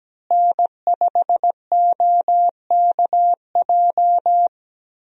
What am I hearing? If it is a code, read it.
Morse code, heard as a keyed tone, about 17 wpm, 700 Hz. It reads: N5OKJ
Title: USA Call Signs - 5 Characters - Repeated 2x - 17wpm